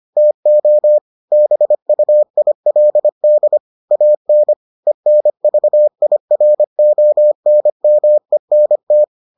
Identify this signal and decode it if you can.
Morse code, heard as a keyed tone, about 25 wpm, 600 Hz. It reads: TO BUILD AN ENVIRONMENT